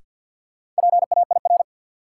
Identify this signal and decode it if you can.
Morse code, heard as a keyed tone, about 52 wpm, 700 Hz. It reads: PAIR
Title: Sentences from Top 1200 Words - New Word Drill - 50wpm